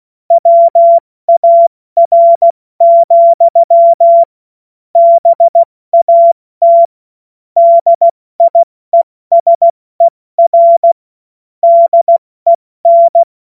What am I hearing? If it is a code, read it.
Morse code, heard as a keyed tone, about 16 wpm, 680 Hz. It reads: WAR, BAT DIESER DEN
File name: Morsekurs / Witze / Witz 09. Dezember 2025 70 F80